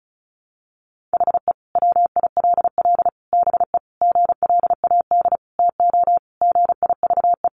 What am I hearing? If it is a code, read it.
Morse code, heard as a keyed tone, about 34 wpm, 700 Hz. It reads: HE WILL BE GLAD TO GIVE
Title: Sentences from Top 800 Words 35wpm